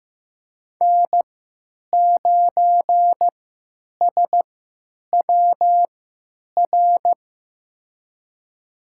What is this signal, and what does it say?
Morse code, heard as a keyed tone, about 15 wpm, spaced as for 9 wpm, 700 Hz. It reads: N9SWR
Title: USA Call Signs - 5 Characters - 3x Character Spacing - 15wpm